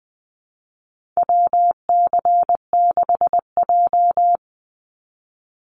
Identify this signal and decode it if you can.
Morse code, heard as a keyed tone, about 20 wpm, 700 Hz. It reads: WC6J
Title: NA SOTA Chaser Callsigns 20wpm